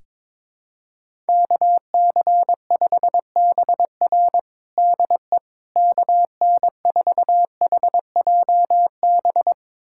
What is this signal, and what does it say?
Morse code, heard as a keyed tone, about 22 wpm, 700 Hz. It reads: KC5BR DE KN4HJB
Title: USA Callsign DE Callsign 22wpm